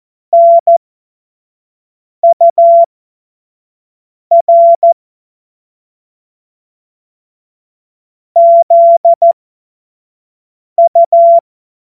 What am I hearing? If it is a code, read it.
Morse code, heard as a keyed tone, about 14 wpm, spaced as for 5 wpm, 680 Hz. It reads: NUR ZU